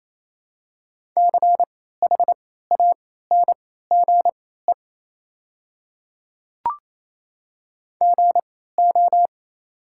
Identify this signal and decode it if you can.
Morse code, heard as a keyed tone, about 28 wpm, spaced as for 16 wpm, 700 Hz. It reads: CHANGE GO
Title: 3x Character Spacing - Top 500 Words - 28wpm